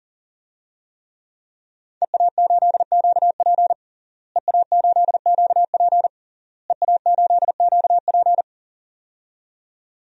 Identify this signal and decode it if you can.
Morse code, heard as a keyed tone, about 40 wpm, 700 Hz. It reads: EA8QP EA8QP EA8QP
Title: DXCC-100 Call Signs - 5 Characters - Repeated 3x - 40wpm